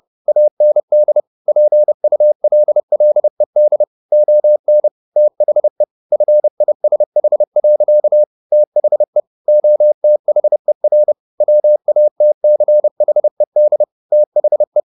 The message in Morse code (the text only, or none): AND PULLED ON THE FISH. THE OTHER WATCHED THE